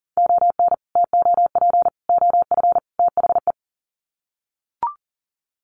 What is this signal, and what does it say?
Morse code, heard as a keyed tone, about 40 wpm, 700 Hz. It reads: ON TOP OF THE
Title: Sets of 4 Words 40wpm